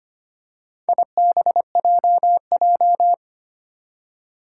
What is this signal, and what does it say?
Morse code, heard as a keyed tone, about 25 wpm, 700 Hz. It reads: IBJJ
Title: Four Challenging Letter-Number Combinations 25wpm